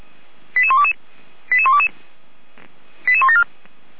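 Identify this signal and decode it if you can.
Five-tone selective call (ZVEI), heard as quick runs of five tones, tones of about 70 ms. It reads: 88122 88122 88174
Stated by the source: ZVEI five-tone selective call 88205, received by SDR and logged by fire brigade feuerwehr-orlamuende.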